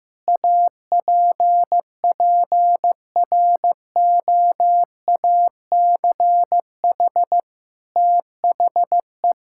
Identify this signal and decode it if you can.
Morse code, heard as a keyed tone, about 15 wpm, 700 Hz. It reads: APPROACH THE
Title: Sentences from Top 1000 Words - Review Part 6 - 15wpm